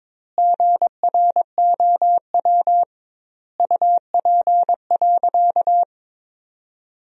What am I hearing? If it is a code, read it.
Morse code, heard as a keyed tone, about 22 wpm, 700 Hz. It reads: GROW UP.
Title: 2x Word-Spacing - Sentences from Top 300 Words - 22wpm